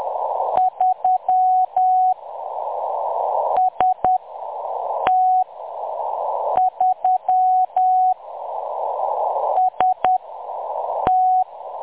Morse code, heard as a keyed tone, about 10 wpm, spaced as for 6 wpm, 740 Hz. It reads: 3 ST3 ST
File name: M23-ST3- -0858utc-10184khz